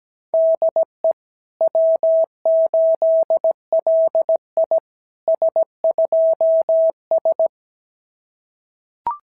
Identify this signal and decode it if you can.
Morse code, heard as a keyed tone, about 16 wpm, 650 Hz. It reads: DE W8LI S2S
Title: SOTA QSO Practice 17wpm